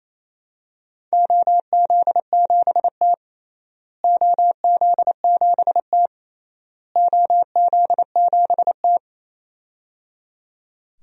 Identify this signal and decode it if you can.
Morse code, heard as a keyed tone, about 28 wpm, 700 Hz. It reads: OZ7T OZ7T OZ7T
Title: DXCC-100 Call Signs - 4 Characters - Repeated 3x - 28wpm